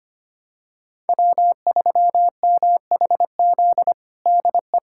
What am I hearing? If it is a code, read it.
Morse code, heard as a keyed tone, about 25 wpm, 700 Hz. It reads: W3MHZ DE
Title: USA Callsign DE Callsign - 5 Characters - 25wpm